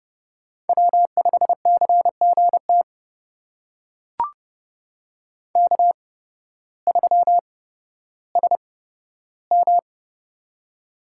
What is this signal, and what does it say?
Morse code, heard as a keyed tone, about 30 wpm, spaced as for 14 wpm, 700 Hz. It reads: W5CGT K3SM
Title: USA Call Signs - 8x Character Spacing - 30wpm